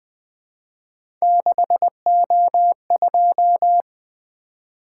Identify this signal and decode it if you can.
Morse code, heard as a keyed tone, about 20 wpm, 700 Hz. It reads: 6O2
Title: Three Letter-Number Combinations - 20wpm